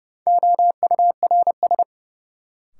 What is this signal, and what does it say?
Morse code, heard as a keyed tone, about 31 wpm, 700 Hz. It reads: OURS
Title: Sentences from Top 1300 Words - New Word Drill Alt - 30wpm